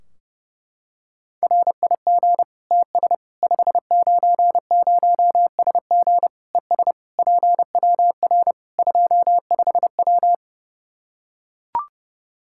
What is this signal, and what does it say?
Morse code, heard as a keyed tone, about 30 wpm, 700 Hz. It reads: RIG TS 590SG ES PWR 25W
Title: Practice QSO - Exchange 2 - Info - 30wpm